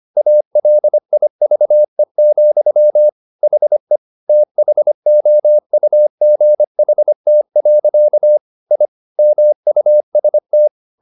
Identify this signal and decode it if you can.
Morse code, heard as a keyed tone, about 25 wpm, 600 Hz. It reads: ALIVE, HE THOUGHT. I MUST